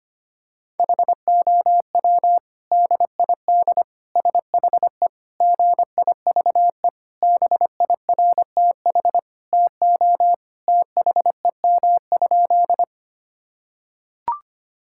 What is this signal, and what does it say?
Morse code, heard as a keyed tone, about 25 wpm, 700 Hz. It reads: HOW DID SHE GIVE BIRTH TO THEM?
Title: Sentences from Top 1000 Words - Review Part 5 - 25wpm